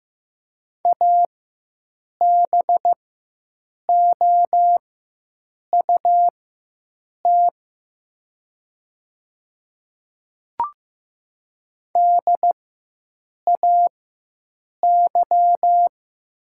Morse code, heard as a keyed tone, about 15 wpm, spaced as for 7 wpm, 700 Hz. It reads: ABOUT DAY
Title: 4x Character Spacing - Top 200 Words - 15wpm